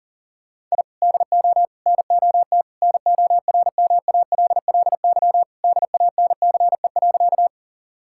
Morse code, heard as a keyed tone, about 40 wpm, 700 Hz. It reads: I DO NOT NORMALLY DANCE.
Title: Sentences from Top 1000 Words - Review Part 6 - 40wpm